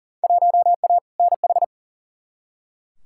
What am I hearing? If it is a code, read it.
Morse code, heard as a keyed tone, about 37 wpm, 700 Hz. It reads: 1A NH
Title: Field Day Exchanges Encore - 40wpm